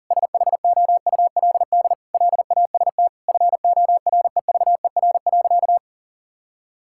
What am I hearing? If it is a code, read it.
Morse code, heard as a keyed tone, about 40 wpm, 700 Hz. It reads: SHOULD LAST FOREVER.
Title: Sentences from Top 1600 Words 40wpm